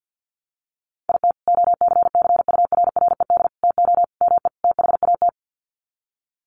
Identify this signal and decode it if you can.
Morse code, heard as a keyed tone, about 50 wpm, 700 Hz. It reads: IT OCCURRED TO ME THAT
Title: Sets of 5 Words Encore 50wpm